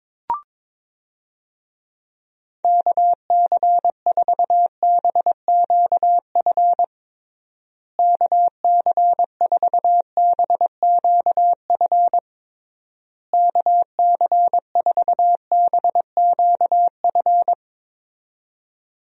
Morse code, heard as a keyed tone, about 22 wpm, 700 Hz. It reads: KC4BQF KC4BQF KC4BQF
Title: USA Call Signs - 6 Characters - Repeated 3x - 22wpm